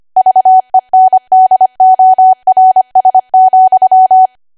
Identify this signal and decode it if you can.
Morse code, heard as a keyed tone, about 25 wpm, 750 Hz. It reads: VENDORS,
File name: cwa / W1AW-Level3 / 25WPM